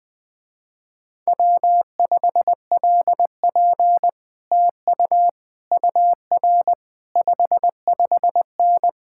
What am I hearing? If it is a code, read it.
Morse code, heard as a keyed tone, about 20 wpm, 700 Hz. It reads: W5LP TU UR 55N
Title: Speed-Racing Common QSO Phrases - 20wpm